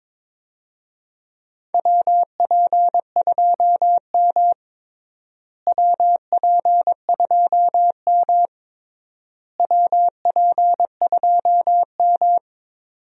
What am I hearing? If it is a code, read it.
Morse code, heard as a keyed tone, about 22 wpm, 700 Hz. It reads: WP2M WP2M WP2M
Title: USA Call Signs - 4 Characters - Repeated 3x - 22wpm